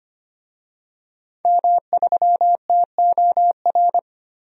Morse code, heard as a keyed tone, about 25 wpm, 700 Hz. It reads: M3TOR
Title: DXCC-100 Call Signs - 5 Characters - Repeated 3x - 25wpm